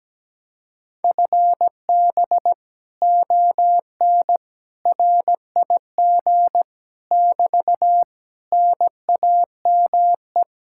Morse code, heard as a keyed tone, about 17 wpm, 700 Hz. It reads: FB ON RIG <BT> NAME
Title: Speed-Racing Common QSO Phrases Encore - 17wpm